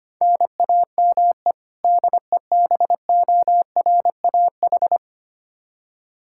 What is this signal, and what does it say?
Morse code, heard as a keyed tone, about 25 wpm, 700 Hz. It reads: NAME DEBORAH
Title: Practice QSO - Exchange 1 - Beginning - 25wpm